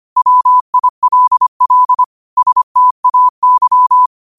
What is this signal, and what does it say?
Morse code, heard as a keyed tone, about 25 wpm, 1000 Hz. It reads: WILL STAY
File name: qotdfiles / 25wpm